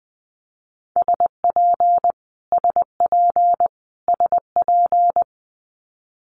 Morse code, heard as a keyed tone, about 19 wpm, 700 Hz. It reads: SP SP SP
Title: Multi-letter Phonemes Repeated 3x 20wpm